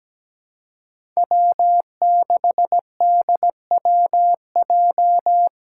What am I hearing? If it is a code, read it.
Morse code, heard as a keyed tone, about 17 wpm, 700 Hz. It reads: W6DWJ
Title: Five Challenging Letter-Number Combinations 17wpm